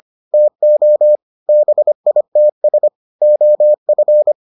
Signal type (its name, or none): Morse code